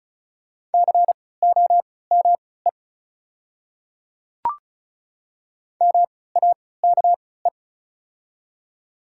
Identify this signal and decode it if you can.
Morse code, heard as a keyed tone, about 36 wpm, spaced as for 20 wpm, 700 Hz. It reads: COME MAKE
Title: 3x Character Spacing - Top 100 Words - 35wpm